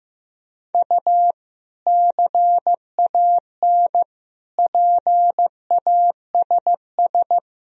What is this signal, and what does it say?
Morse code, heard as a keyed tone, about 16 wpm, 700 Hz. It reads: U CAN PASS
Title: Sentences from Top 400 Words and Abbreviations 15wpm